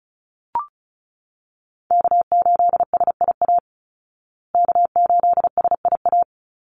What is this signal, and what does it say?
Morse code, heard as a keyed tone, about 34 wpm, 700 Hz. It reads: K8SIA K8SIA
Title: Call Signs Repeated 2x - 35wpm